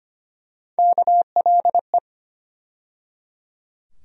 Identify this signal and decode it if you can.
Morse code, heard as a keyed tone, about 25 wpm, 700 Hz. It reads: KLE
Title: USA Call Sign Postfixes - 3 Letters 25wpm